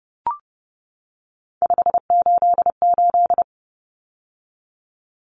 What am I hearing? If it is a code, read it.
Morse code, heard as a keyed tone, about 30 wpm, 700 Hz. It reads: 588
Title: Numbers - 3-Digit 30wpm